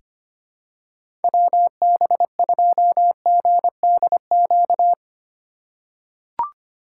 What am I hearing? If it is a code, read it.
Morse code, heard as a keyed tone, about 25 wpm, 700 Hz. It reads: WB2GDQ
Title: USA Call Signs - 6 Characters - 4x Character Spacing - 25wpm